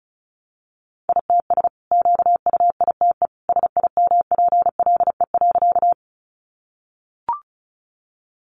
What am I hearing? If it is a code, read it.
Morse code, heard as a keyed tone, about 34 wpm, 700 Hz. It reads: ITS QUITE SIMPLE.
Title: Sentences from Top 700 Words 35wpm